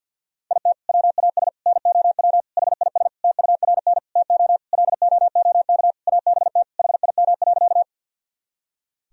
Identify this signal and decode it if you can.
Morse code, heard as a keyed tone, about 52 wpm, 700 Hz. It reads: IT WAS NOW HIS TURN TO LOOK ABT HIM.
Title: Sentences from Top 200 Words and Abbreviations 50wpm